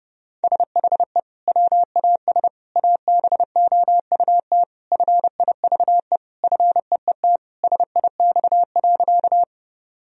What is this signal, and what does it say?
Morse code, heard as a keyed tone, about 30 wpm, 700 Hz. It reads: SHE WAS ABOUT FIVE FEET SIX.
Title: Sentences from Top 1000 Words - Review Part 2 - 30wpm